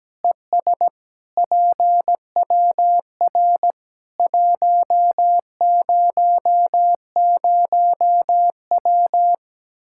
Morse code, heard as a keyed tone, about 17 wpm, 700 Hz. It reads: ES PWR 100W
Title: Practice QSO - Exchange 2 - Info - 17wpm